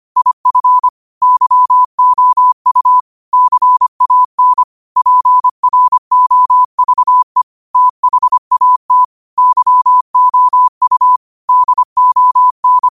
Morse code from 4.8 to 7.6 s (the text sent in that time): PROVE